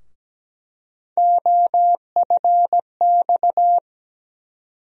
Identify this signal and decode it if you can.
Morse code, heard as a keyed tone, about 17 wpm, 700 Hz. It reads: OFX
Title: USA Call Sign Postfixes - 3 Letters 17wpm